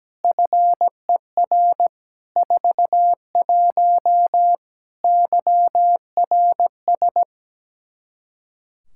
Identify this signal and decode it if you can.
Morse code, heard as a keyed tone, about 17 wpm, 700 Hz. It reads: FER 41 YRS
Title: Practice QSO - Exchange 3 - More Info - 17wpm